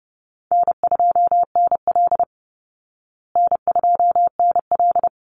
Morse code, heard as a keyed tone, about 30 wpm, 700 Hz. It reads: N2NL N2NL
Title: Easy Call Signs Repeated 2x - 30wpm